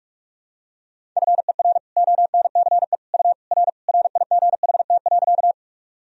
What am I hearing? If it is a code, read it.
Morse code, heard as a keyed tone, about 42 wpm, 700 Hz. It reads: FER ONCE U R RIGHT.
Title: Sentences from Top 300 Words and Abbreviations 45wpm